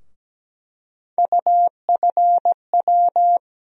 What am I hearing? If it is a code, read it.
Morse code, heard as a keyed tone, about 17 wpm, 700 Hz. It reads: UFW
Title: USA Call Sign Postfixes - 3 Letters 17wpm